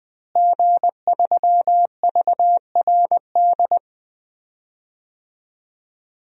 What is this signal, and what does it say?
Morse code, heard as a keyed tone, about 20 wpm, 700 Hz. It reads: G3VRD
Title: DXCC-100 Call Signs - 5 Characters - Repeated 2x - 20wpm